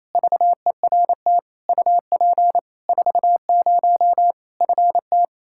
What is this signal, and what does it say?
Morse code, heard as a keyed tone, about 28 wpm, 700 Hz. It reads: VERT UP 40 FT
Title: K1USN SST QSO Practice 28wpm